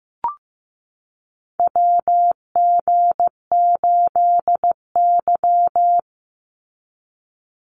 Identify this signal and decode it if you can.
Morse code, heard as a keyed tone, about 15 wpm, 700 Hz. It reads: WG8Y
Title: NA SOTA Chaser Callsigns 15wpm